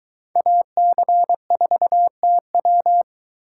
Morse code, heard as a keyed tone, about 22 wpm, 700 Hz. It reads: AC4TW